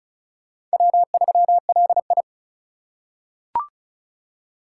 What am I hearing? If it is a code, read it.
Morse code, heard as a keyed tone, about 34 wpm, 700 Hz. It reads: W3LI
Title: USA Call Signs - 4 Characters - 4x Character Spacing - 35wpm